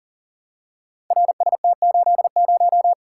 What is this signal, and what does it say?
Morse code, heard as a keyed tone, about 43 wpm, 700 Hz. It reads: RST80
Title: Code Groups - 5 Letter:Numbers - 40wpm